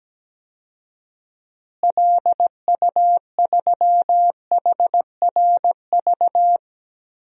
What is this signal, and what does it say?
Morse code, heard as a keyed tone, about 17 wpm, 700 Hz. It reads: LU3HRV
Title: DXCC-100 Call Signs - 6 Characters - Repeated 3x - 17wpm